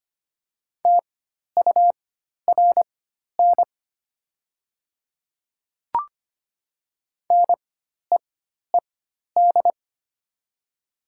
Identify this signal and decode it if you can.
Morse code, heard as a keyed tone, about 25 wpm, spaced as for 12 wpm, 700 Hz. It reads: TURN NEED
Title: 4x Character Spacing - Top 200 Words - 25wpm